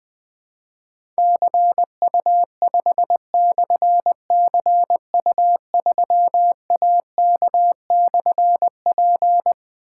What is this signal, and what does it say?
Morse code, heard as a keyed tone, about 21 wpm, 700 Hz. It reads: CU5/CU3AK/P
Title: Difficult Call Signs 20wpm